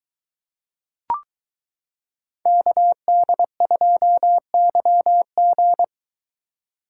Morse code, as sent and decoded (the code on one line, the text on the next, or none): -.- -.. ..--- -.-- --.
KD2YG